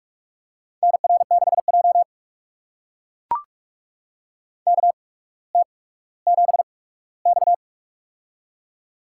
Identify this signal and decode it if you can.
Morse code, heard as a keyed tone, about 44 wpm, spaced as for 19 wpm, 700 Hz. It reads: NR6J KT7X
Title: USA Call Signs - 4 Characters - 8x Character Spacing - 45wpm